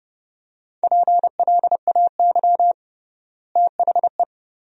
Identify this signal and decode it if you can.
Morse code, heard as a keyed tone, about 30 wpm, 700 Hz. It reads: PLAY THE
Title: 3x Word-Spacing - Sentences from Top 500 Words - 30wpm